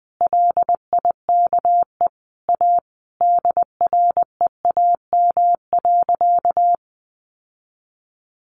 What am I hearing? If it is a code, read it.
Morse code, heard as a keyed tone, about 20 wpm, 700 Hz. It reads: LIKE A DREAM.
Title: Sentences from Top 1500 Words 20wpm